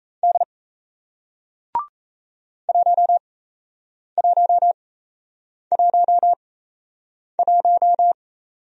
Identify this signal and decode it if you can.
Morse code, heard as a keyed tone, about 38 wpm, spaced as for 8 wpm, 700 Hz. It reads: D 1111